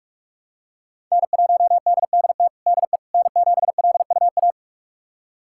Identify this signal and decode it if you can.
Morse code, heard as a keyed tone, about 41 wpm, 700 Hz. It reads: N1DDT DE N7LUA
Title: USA Callsign DE Callsign - 5 Characters - 45wpm